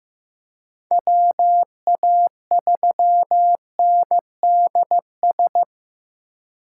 Morse code, heard as a keyed tone, about 15 wpm, 700 Hz. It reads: WA3NDS
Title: USA Call Signs - 6 Characters - 3x Character Spacing - 15wpm